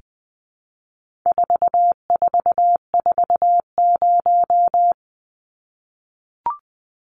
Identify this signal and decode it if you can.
Morse code, heard as a keyed tone, about 20 wpm, 700 Hz. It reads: 4440
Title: Numbers - 4-Digit 20wpm